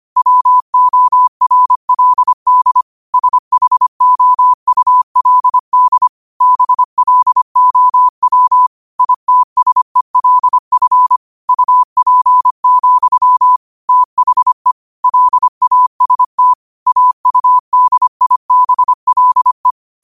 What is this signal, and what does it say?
Morse code, heard as a keyed tone, about 25 wpm, 1000 Hz. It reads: WORLD SHOULD BLOW ITSELF UP, THE LAST AUDIBLE